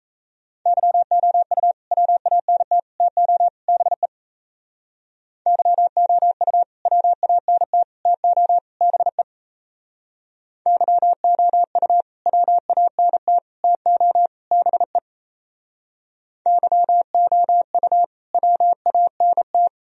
Morse code, heard as a keyed tone, about 34 wpm, 700 Hz. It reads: YOU WANT TO BE YOU WANT TO BE YOU WANT TO BE YOU WANT
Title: Speed-Racing - Sets of 4 Words 28wpm